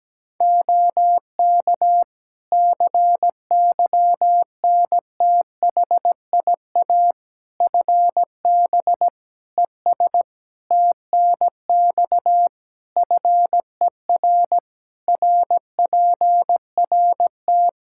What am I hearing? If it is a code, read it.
Morse code, heard as a keyed tone, about 17 wpm, 700 Hz. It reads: OK CYNTHIA FB ES TNX FER RPRT
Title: K1USN SST QSO Practice 17wpm